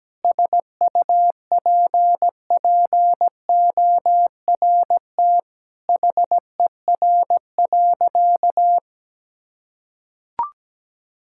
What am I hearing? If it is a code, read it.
Morse code, heard as a keyed tone, about 17 wpm, 700 Hz. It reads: SUPPORT HER.
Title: Sentences from Top 1000 Words - Review Part 5 - 17wpm